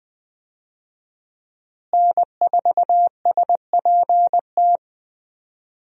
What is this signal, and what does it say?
Morse code, heard as a keyed tone, about 20 wpm, 700 Hz. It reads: N4SPT
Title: USA Call Signs - 5 Characters - Repeated 2x - 20wpm